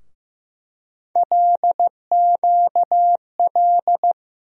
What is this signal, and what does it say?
Morse code, heard as a keyed tone, about 16 wpm, 700 Hz. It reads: LQL
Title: USA Call Sign Postfixes - 3 Letters 15wpm